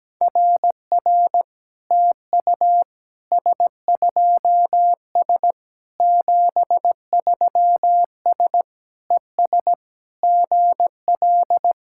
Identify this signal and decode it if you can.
Morse code, heard as a keyed tone, about 17 wpm, 700 Hz. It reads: RR TU S2S 73S ES GL
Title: SOTA QSO Practice 17wpm